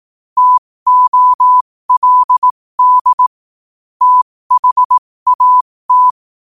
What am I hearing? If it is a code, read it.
Morse code, heard as a keyed tone, about 18 wpm, spaced as for 15 wpm, 1000 Hz. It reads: TOLD THAT
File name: qotdfiles / 15wpm